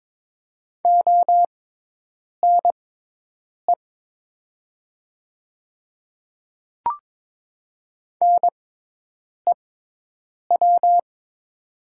Morse code, heard as a keyed tone, about 22 wpm, spaced as for 8 wpm, 700 Hz. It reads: ONE NEW
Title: 6x Character Spacing - Top 200 Words - 22wpm